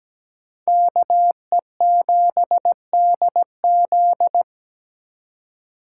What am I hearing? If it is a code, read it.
Morse code, heard as a keyed tone, about 20 wpm, 700 Hz. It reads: KE7DZ
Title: Speed Racing - USA 2x2 Call Signs 17wpm